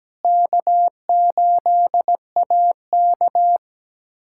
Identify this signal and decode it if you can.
Morse code, heard as a keyed tone, about 17 wpm, 700 Hz. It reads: K8AK
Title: USA Callsign DE Callsign - 4 Characters - 17wpm